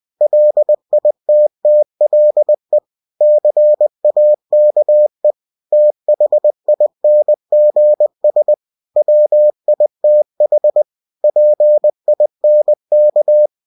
Morse code, heard as a keyed tone, about 20 wpm, 600 Hz. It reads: LITTLE CAKE THINGS WITH PINK
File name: cw-books / winnie-the-pooh / winnie-20wpm-page-0103